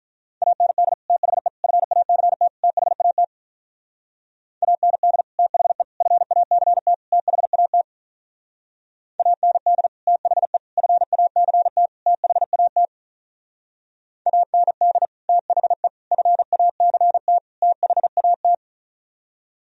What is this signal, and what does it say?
Morse code, heard as a keyed tone, about 42 wpm, 700 Hz. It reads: AND THE FACT THAT AND THE FACT THAT AND THE FACT THAT AND THE FACT THAT
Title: Speed-Racing - Sets of 4 Words 35wpm